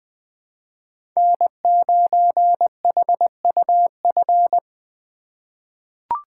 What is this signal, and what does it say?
Morse code, heard as a keyed tone, about 20 wpm, 700 Hz. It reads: N9HUF
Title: USA Call Signs - 5 Characters - 2x Character Spacing - 20wpm